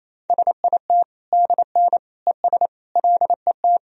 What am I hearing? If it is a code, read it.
Morse code, heard as a keyed tone, about 28 wpm, 700 Hz. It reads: SIT DN ES LET
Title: Sentences from Top 500 Words and Abbreviations 28wpm